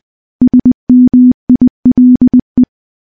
Morse code, heard as a keyed tone, about 20 wpm, 250 Hz. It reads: SMILE
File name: morse / morse-training-20WPM